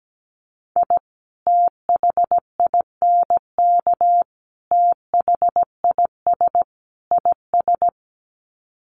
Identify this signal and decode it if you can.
Morse code, heard as a keyed tone, about 17 wpm, 700 Hz. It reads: I THINK THIS IS
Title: Sets of 4 Words 17wpm